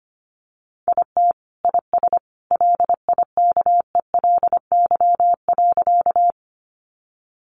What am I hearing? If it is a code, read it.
Morse code, heard as a keyed tone, about 25 wpm, 700 Hz. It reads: IT IS LIKELY.
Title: Sentences from Top 600 Words 25wpm